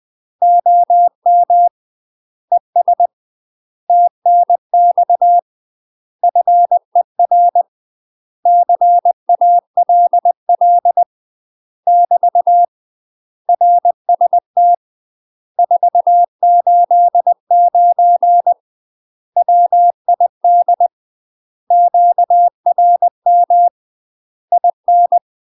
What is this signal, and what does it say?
Morse code, heard as a keyed tone, about 20 wpm, 700 Hz. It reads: OM ES TNX FER CALL <BT> RST 489 WID QRM IN
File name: UK MASTER QSOs TRAINING / 20WPM / UK MASTER QSOs 0038